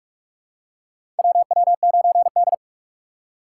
Morse code, heard as a keyed tone, about 42 wpm, 700 Hz. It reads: WW9D
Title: USA Call Signs - 3x Character Spacing - 45wpm